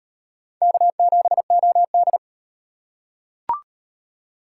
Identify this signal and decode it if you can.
Morse code, heard as a keyed tone, about 38 wpm, 700 Hz. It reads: K7OD